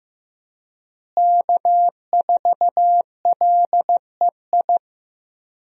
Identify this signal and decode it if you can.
Morse code, heard as a keyed tone, about 15 wpm, 700 Hz. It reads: K4LEI
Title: USA Call Signs - 5 Characters - Repeated 3x - 15wpm